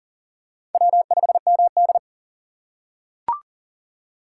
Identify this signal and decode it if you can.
Morse code, heard as a keyed tone, about 38 wpm, 690 Hz. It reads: W5MD
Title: USA Call Signs - 4 Characters - 2x Character Spacing - 40wpm